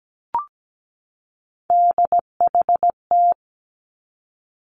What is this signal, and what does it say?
Morse code, heard as a keyed tone, about 17 wpm, 700 Hz. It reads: DHT
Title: Three Challenging Letter-Number Combinations 17wpm